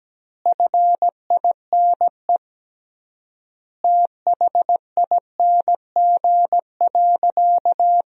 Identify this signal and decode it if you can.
Morse code, heard as a keyed tone, about 17 wpm, 700 Hz. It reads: FINE THING.
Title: 3x Word-Spacing - Sentences from Top 500 Words - 17wpm